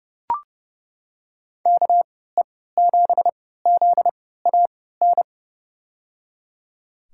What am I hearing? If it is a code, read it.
Morse code, heard as a keyed tone, about 32 wpm, spaced as for 17 wpm, 700 Hz. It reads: KE7ZAN
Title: USA Call Signs - 3x Character Spacing - 30wpm